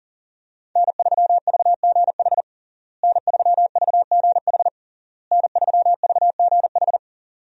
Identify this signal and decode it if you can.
Morse code, heard as a keyed tone, about 40 wpm, 700 Hz. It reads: N3VGH N3VGH N3VGH
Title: USA Call Signs - 5 Characters - Repeated 3x - 40wpm